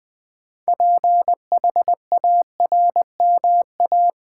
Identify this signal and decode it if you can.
Morse code, heard as a keyed tone, about 20 wpm, 700 Hz. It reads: PHARMA
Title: Build A Word - 2 Letters At A Time - 8-Letter Words - 20wpm